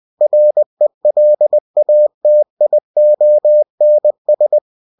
Morse code, heard as a keyed tone, about 20 wpm, 600 Hz. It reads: RELATIONS
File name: cw-books / winnie-the-pooh / winnie-20wpm-page-0020